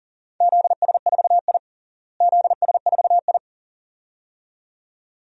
Morse code, heard as a keyed tone, about 40 wpm, 700 Hz. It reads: ZS4I ZS4I
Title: DXCC-100 Call Signs - 4 Characters - Repeated 2x - 40wpm